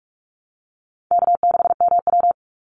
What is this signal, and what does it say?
Morse code, heard as a keyed tone, about 44 wpm, 700 Hz. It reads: K6MW
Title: NA SOTA Chaser Callsigns 45wpm